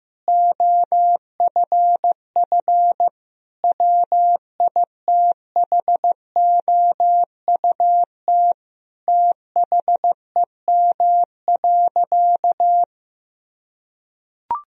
Morse code, heard as a keyed tone, about 14 wpm, 700 Hz. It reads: OFF WITHOUT THEM.
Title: Sentences from Top 1000 Words - Review Part 3 - 15wpm